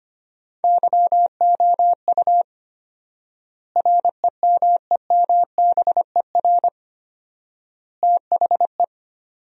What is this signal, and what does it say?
Morse code, heard as a keyed tone, about 25 wpm, 700 Hz. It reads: YOU REMEMBER THE
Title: 4x Word-Spacing - Sentences from Top 400 Words - 25wpm